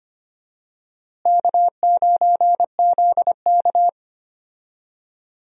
Morse code, heard as a keyed tone, about 25 wpm, 700 Hz. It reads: K9ZK
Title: USA Call Signs - 4 Characters - Repeated 2x - 25wpm